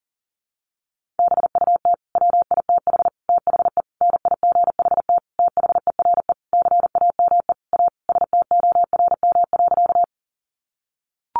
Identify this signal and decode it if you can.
Morse code, heard as a keyed tone, about 40 wpm, 700 Hz. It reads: BUT WITH THE NIGHT THERE CAME A STORM.
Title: Sentences from Top 1500 Words 40wpm